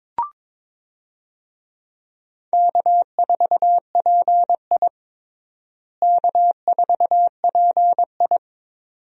Morse code, heard as a keyed tone, about 21 wpm, 700 Hz. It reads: K4PI K4PI
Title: USA Call Signs - 4 Characters - Repeated 2x - 22wpm